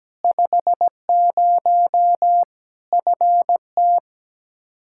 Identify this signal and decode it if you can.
Morse code, heard as a keyed tone, about 17 wpm, 700 Hz. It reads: 50 FT
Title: Practice QSO - Exchange 2 - Info - 17wpm